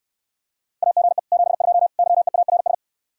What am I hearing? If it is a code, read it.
Morse code, heard as a keyed tone, about 67 wpm, 700 Hz. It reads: AGE 72 YEARS